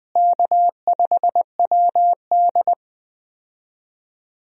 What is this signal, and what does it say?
Morse code, heard as a keyed tone, about 20 wpm, 700 Hz. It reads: K5WD
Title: USA Call Signs - 4 Characters - Repeated 3x - 20wpm